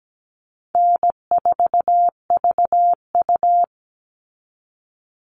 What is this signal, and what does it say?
Morse code, heard as a keyed tone, about 17 wpm, 700 Hz. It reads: N4VU
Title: All Cut Numbers - 4-Digit 17wpm